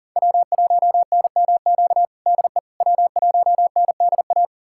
Morse code, heard as a keyed tone, about 40 wpm, 700 Hz. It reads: W1NMQ DE W1NDA
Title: USA Callsign DE Callsign 40wpm